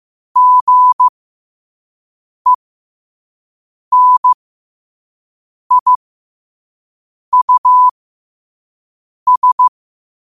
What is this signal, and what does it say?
Morse code, heard as a keyed tone, about 15 wpm, spaced as for 5 wpm, 1000 Hz. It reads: GENIUS
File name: qotdfiles / 5wpm